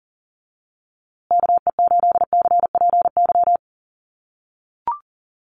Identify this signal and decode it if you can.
Morse code, heard as a keyed tone, about 40 wpm, 700 Hz. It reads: KE8CPY
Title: USA Call Signs Repeated 3x 40wpm